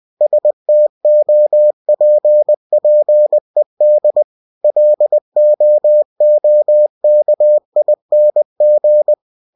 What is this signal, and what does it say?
Morse code, heard as a keyed tone, about 20 wpm, 600 Hz. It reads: STOPPED LOOKING
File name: cw-books / winnie-the-pooh / winnie-20wpm-page-0054